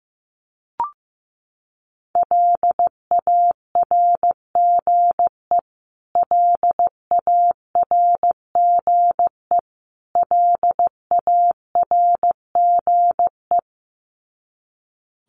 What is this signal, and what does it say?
Morse code, heard as a keyed tone, about 15 wpm, 700 Hz. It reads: LARGE LARGE LARGE
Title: Top 500 Words Repeated 15wpm